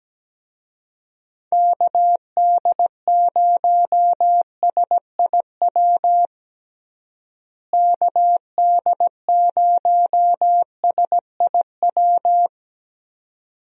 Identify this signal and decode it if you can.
Morse code, heard as a keyed tone, about 17 wpm, 700 Hz. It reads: KD0SIW KD0SIW
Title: USA Call Signs - 6 Characters - Repeated 2x - 17wpm